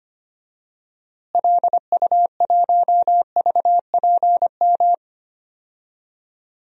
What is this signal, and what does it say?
Morse code, heard as a keyed tone, about 25 wpm, 700 Hz. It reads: LU1VPM
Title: DXCC-100 Call Signs - 6 Characters - Repeated 2x - 25wpm